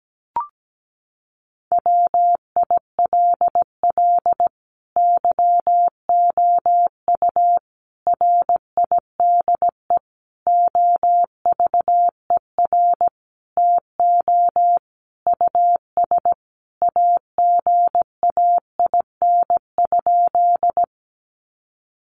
Morse code, heard as a keyed tone, about 17 wpm, 700 Hz. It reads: WILL YOU RIDE OVER TO US AGAIN?
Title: Sentences from Top 1200 Words 17wpm